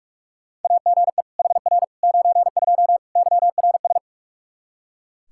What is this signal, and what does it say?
Morse code, heard as a keyed tone, about 41 wpm, 700 Hz. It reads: AGE HR 92 YRS
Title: Practice QSO - Exchange 3 - More Info - 45wpm